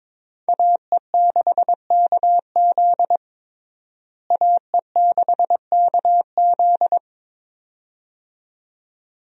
Morse code, heard as a keyed tone, about 22 wpm, 700 Hz. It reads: AE6KZ AE6KZ
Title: USA Call Signs - 5 Characters - Repeated 2x - 22wpm